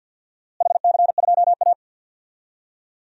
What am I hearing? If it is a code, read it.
Morse code, heard as a keyed tone, about 52 wpm, 700 Hz. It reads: HC2A
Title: DXCC-100 Call Signs - 4 Characters - Repeated 3x - 50wpm